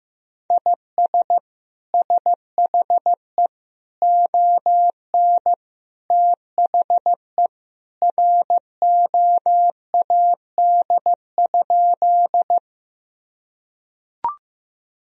Morse code, heard as a keyed tone, about 15 wpm, 700 Hz. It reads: IS SHE ON THE ROAD?
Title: Sentences from Top 1000 Words - Review Part 2 - 15wpm